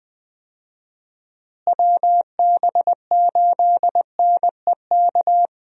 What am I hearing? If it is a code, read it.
Morse code, heard as a keyed tone, about 21 wpm, 700 Hz. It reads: WB8NEK
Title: USA Call Signs - 6 Characters - Repeated 2x - 20wpm